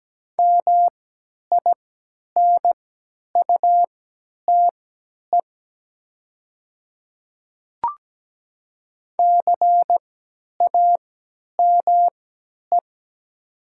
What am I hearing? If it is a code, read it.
Morse code, heard as a keyed tone, about 17 wpm, spaced as for 10 wpm, 690 Hz. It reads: MINUTE CAME
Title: 3x Character Spacing - Top 500 Words - 17wpm